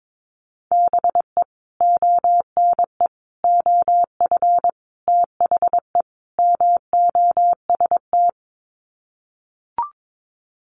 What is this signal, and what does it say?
Morse code, heard as a keyed tone, about 22 wpm, 700 Hz. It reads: BE ONE OF THE MOST
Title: Sets of 5 Words Encore 22wpm